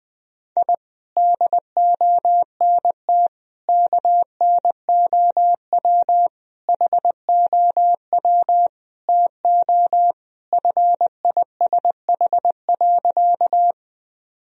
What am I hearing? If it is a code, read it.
Morse code, heard as a keyed tone, about 20 wpm, 700 Hz. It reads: I DONT KNOW HOW TO FISH.
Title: Sentences from Top 1000 Words - Review Part 1 - 20wpm